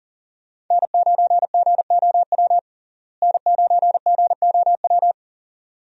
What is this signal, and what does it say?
Morse code, heard as a keyed tone, about 40 wpm, 700 Hz. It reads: N9GOW N9GOW
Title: USA Call Signs - 5 Characters - Repeated 2x - 40wpm